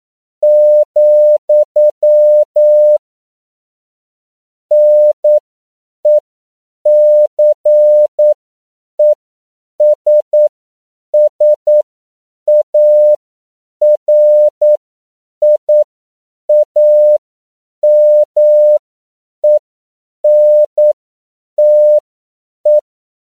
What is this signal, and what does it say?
Morse code, heard as a keyed tone, about 9 wpm, spaced as for 7 wpm, 600 Hz. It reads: , NECESSARIAMENTE
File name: CW / 10PPM